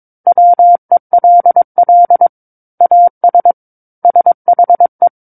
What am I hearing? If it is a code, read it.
Morse code, heard as a keyed tone, about 22 wpm, 700 Hz. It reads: WELL AS SHE
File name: 06-Chapter-06 22 WPM